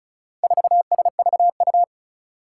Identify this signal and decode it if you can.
Morse code, heard as a keyed tone, about 36 wpm, 700 Hz. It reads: 4SVU
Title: Four Challenging Letter-Number Combinations 35wpm